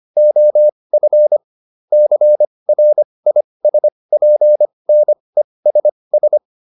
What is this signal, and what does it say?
Morse code, heard as a keyed tone, about 25 wpm, spaced as for 20 wpm, 600 Hz. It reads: OF CRISPNESS